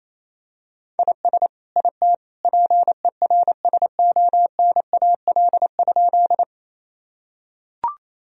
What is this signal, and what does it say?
Morse code, heard as a keyed tone, about 27 wpm, 700 Hz. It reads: IS IT PERSONAL?
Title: Sentences from Top 1000 Words - Review Part 4 - 28wpm